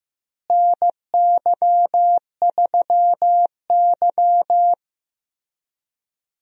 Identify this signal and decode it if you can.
Morse code, heard as a keyed tone, about 15 wpm, 700 Hz. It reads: NY3Y
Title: USA Call Signs - 4 Characters - Repeated 3x - 15wpm